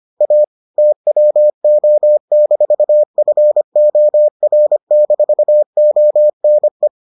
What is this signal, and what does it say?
Morse code, heard as a keyed tone, about 25 wpm, 600 Hz. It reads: A TWO-FOR-ONE